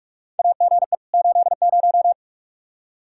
Morse code, heard as a keyed tone, about 46 wpm, 700 Hz. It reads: AGE 80
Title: K1USN SST QSO Practice 45wpm